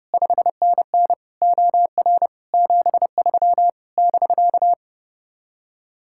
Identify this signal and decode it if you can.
Morse code, heard as a keyed tone, about 34 wpm, 700 Hz. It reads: 5NN OR 73 <BK>